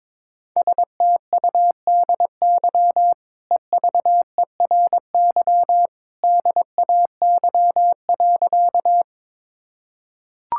Morse code, heard as a keyed tone, about 22 wpm, 700 Hz. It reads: STUDY EVERY DAY.
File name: practice / 22wpm / Easy Sentences with Hints - 22wpm